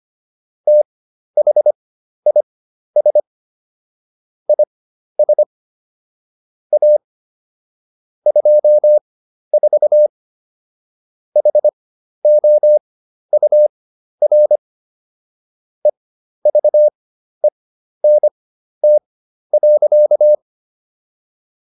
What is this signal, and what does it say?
Morse code, heard as a keyed tone, about 25 wpm, spaced as for 12 wpm, 600 Hz. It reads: THIS IS A 24 HOUR EVENT.